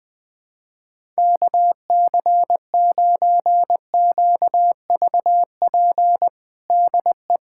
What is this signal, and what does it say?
Morse code, heard as a keyed tone, about 20 wpm, 700 Hz. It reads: KC9QVP DE
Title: USA Callsign DE Callsign - 6 Characters - 20wpm